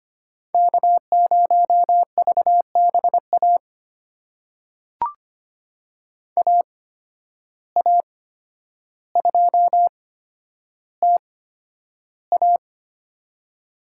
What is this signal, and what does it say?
Morse code, heard as a keyed tone, about 25 wpm, spaced as for 11 wpm, 700 Hz. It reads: K0VBA AA2TA
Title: USA Call Signs - 5 Characters - 8x Character Spacing - 25wpm